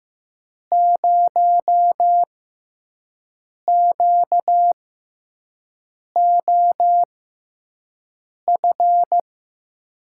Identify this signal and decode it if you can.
Morse code, heard as a keyed tone, about 15 wpm, spaced as for 5 wpm, 700 Hz. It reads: 0QOF
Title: USA Call Signs - 6 Characters - 6x Character Spacing - 15wpm